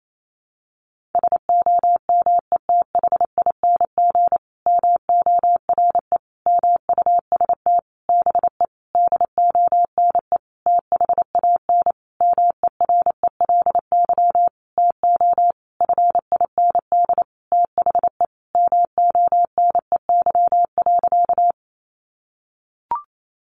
Morse code, heard as a keyed tone, about 28 wpm, 700 Hz. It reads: SOMETHING MORE MUST BE DONE THAN MERELY TO FIND THE MONEY.
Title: Sentences from Top 1400 Words 28wpm